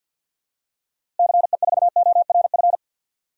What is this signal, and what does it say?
Morse code, heard as a keyed tone, about 49 wpm, 700 Hz. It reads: CE4ORF
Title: DXCC-100 Call Signs - 6 Characters - Repeated 3x - 50wpm